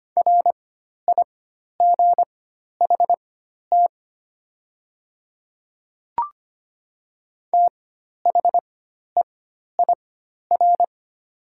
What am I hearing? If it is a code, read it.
Morse code, heard as a keyed tone, about 25 wpm, spaced as for 12 wpm, 700 Hz. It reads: RIGHT THEIR
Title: 4x Character Spacing - Top 200 Words - 25wpm